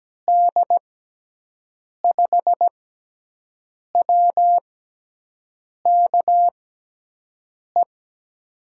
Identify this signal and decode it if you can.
Morse code, heard as a keyed tone, about 17 wpm, spaced as for 6 wpm, 700 Hz. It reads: D5WKE
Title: USA Call Signs - 6 Characters - 6x Character Spacing - 17wpm